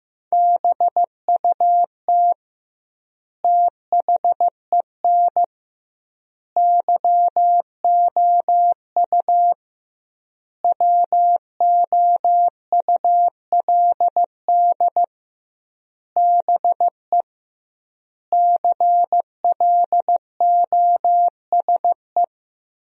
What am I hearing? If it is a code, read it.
Morse code, heard as a keyed tone, about 15 wpm, 700 Hz. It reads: BUT THEN YOU WOULD BE CLOSE
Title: 2x Word-Spacing - Sentences from Top 300 Words - 15wpm